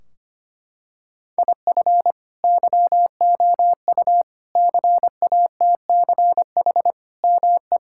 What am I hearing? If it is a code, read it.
Morse code, heard as a keyed tone, about 25 wpm, 700 Hz. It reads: IF YOU CATCH ME
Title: Sentences from Top 1600 Words 25wpm